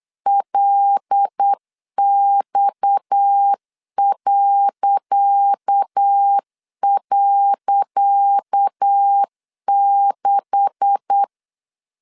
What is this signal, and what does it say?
Morse code, heard as a keyed tone, about 8 wpm, 800 Hz. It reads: LX..6
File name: LEZIONI / 10 SIGNS / 10C DOT 8wpm 3649